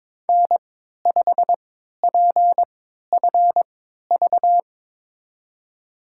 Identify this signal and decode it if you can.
Morse code, heard as a keyed tone, about 22 wpm, spaced as for 12 wpm, 700 Hz. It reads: N5PFV
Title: USA Call Signs - 5 Characters - 3x Character Spacing - 22wpm